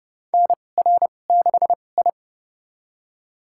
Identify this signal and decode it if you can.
Morse code, heard as a keyed tone, about 32 wpm, spaced as for 22 wpm, 700 Hz. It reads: NR6I
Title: USA Call Signs - 4 Characters - 2x Character Spacing - 30wpm